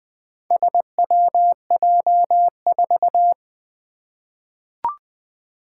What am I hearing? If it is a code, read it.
Morse code, heard as a keyed tone, about 21 wpm, 700 Hz. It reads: SWJ4
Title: Four Challenging Letter-Number Combinations 20wpm